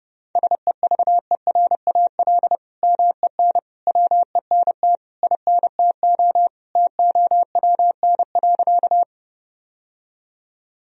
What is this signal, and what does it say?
Morse code, heard as a keyed tone, about 30 wpm, 700 Hz. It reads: SEVERAL MEN WENT INTO TOWN.
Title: Sentences from Top 1000 Words - Review Part 2 - 30wpm